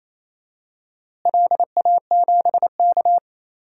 Morse code, heard as a keyed tone, about 28 wpm, 700 Hz. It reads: LA7K
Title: DXCC-100 Call Signs - 4 Characters - Repeated 3x - 28wpm